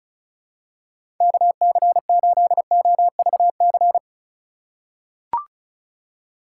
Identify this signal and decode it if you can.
Morse code, heard as a keyed tone, about 34 wpm, 700 Hz. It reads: KC8OVC
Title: USA Call Signs - 6 Characters - 2x Character Spacing - 35wpm